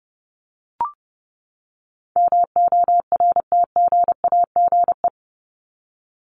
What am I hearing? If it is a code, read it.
Morse code, heard as a keyed tone, about 30 wpm, 700 Hz. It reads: MORTGAGE
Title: Top 1751-2000 Words 30wpm